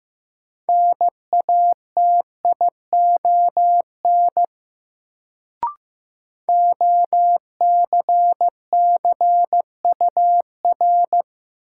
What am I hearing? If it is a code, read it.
Morse code, heard as a keyed tone, about 15 wpm, 700 Hz. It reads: NATION OCCUR
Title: Sentences from Top 900 Words - New Word Drill - 15wpm